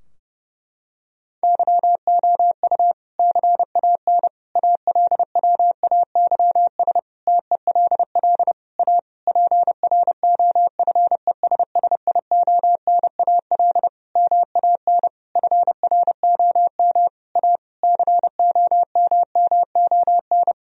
Morse code, heard as a keyed tone, about 30 wpm, 700 Hz. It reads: YOU CAN ALWAYS TELL A PROFESSIONAL MAN FROM A COMMON
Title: Sentences from Top 1000 Words - Review Part 4 - 30wpm